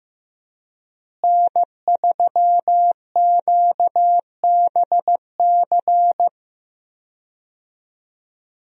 Morse code, heard as a keyed tone, about 15 wpm, 700 Hz. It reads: N3QBC
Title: USA Call Signs - 5 Characters - Repeated 2x - 15wpm